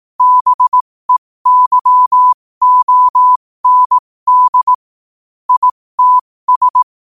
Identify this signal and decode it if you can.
Morse code, heard as a keyed tone, about 18 wpm, spaced as for 15 wpm, 1000 Hz. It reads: BEYOND ITS
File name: qotdfiles / 15wpm